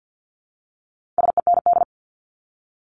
Morse code, heard as a keyed tone, about 50 wpm, 700 Hz. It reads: SEND
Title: Multi-letter Phonemes Repeated 3x 50wpm